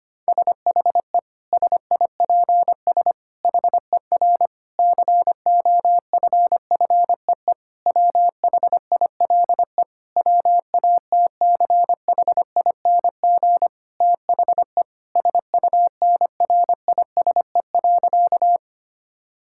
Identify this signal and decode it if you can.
Morse code, heard as a keyed tone, about 25 wpm, 700 Hz. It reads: SHE SIPS HER COFFEE WHILE WATCHING THE SUNRISE.
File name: practice / 25wpm / Difficult Sentences with Hints - 25wpm